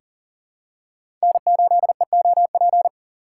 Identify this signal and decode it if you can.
Morse code, heard as a keyed tone, about 40 wpm, 700 Hz. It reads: N8EOP
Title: USA Call Signs - 5 Characters - 3x Character Spacing - 40wpm